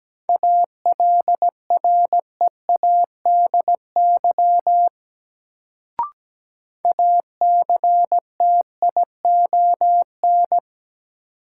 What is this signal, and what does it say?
Morse code, heard as a keyed tone, about 17 wpm, 700 Hz. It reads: ALREADY ACTION
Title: Sentences from Top 700 Words - New Word Drill - 17wpm